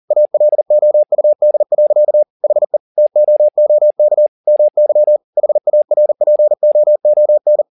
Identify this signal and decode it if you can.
Morse code, heard as a keyed tone, about 40 wpm, 600 Hz. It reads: ALOUD. HE TOOK MY HARPOON
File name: the old man and the sea 2 40wpm